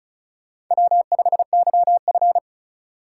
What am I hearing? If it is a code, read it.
Morse code, heard as a keyed tone, about 44 wpm, 700 Hz. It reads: W5YF